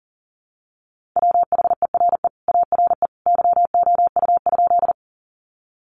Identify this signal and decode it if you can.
Morse code, heard as a keyed tone, about 40 wpm, 700 Hz. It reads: WHERE ARE YOU?
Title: Sentences from Top 200 Words 40wpm